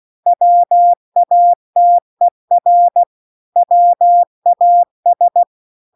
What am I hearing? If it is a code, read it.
Morse code, heard as a keyed tone, about 16 wpm, 700 Hz. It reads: WATER WAS